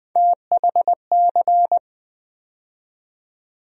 Morse code, heard as a keyed tone, about 20 wpm, 700 Hz. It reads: THC
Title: USA Call Sign Postfixes - 3 Letters 20wpm